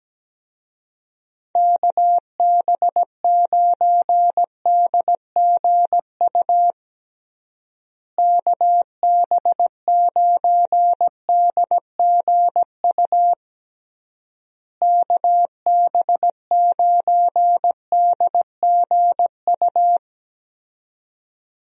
Morse code, heard as a keyed tone, about 17 wpm, 700 Hz. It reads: KB9DGU KB9DGU KB9DGU
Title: USA Call Signs - 6 Characters - Repeated 3x - 17wpm